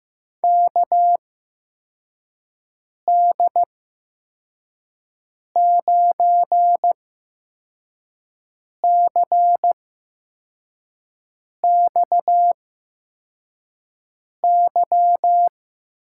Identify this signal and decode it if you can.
Morse code, heard as a keyed tone, about 15 wpm, spaced as for 4 wpm, 700 Hz. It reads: KD9CXY
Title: USA Call Signs - 6 Characters - 8x Character Spacing - 15wpm